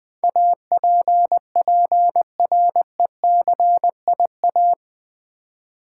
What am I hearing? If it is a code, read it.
Morse code, heard as a keyed tone, about 20 wpm, 700 Hz. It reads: APPRECIA
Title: Build A Word - 4 Letters At A Time - 10-Letter Words - 20wpm